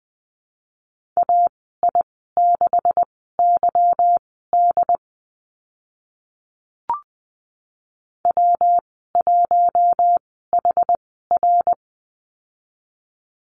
Morse code, heard as a keyed tone, about 20 wpm, spaced as for 15 wpm, 700 Hz. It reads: AI6YD W1HR
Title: 147 Lesson 24g - Letter y - Call Signs 20wpm